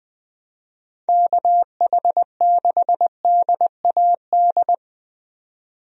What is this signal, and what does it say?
Morse code, heard as a keyed tone, about 20 wpm, 700 Hz. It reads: KH6DAD
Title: DXCC-100 Call Signs - 6 Characters - 20wpm